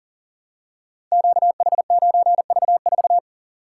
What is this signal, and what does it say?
Morse code, heard as a keyed tone, about 38 wpm, 690 Hz. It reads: QH9V4
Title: Code Groups - 5 Letter:Numbers - 40wpm